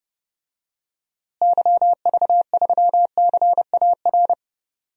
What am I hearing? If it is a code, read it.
Morse code, heard as a keyed tone, about 30 wpm, 700 Hz. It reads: YV3CAR
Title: DXCC-100 Call Signs - 6 Characters - Repeated 2x - 30wpm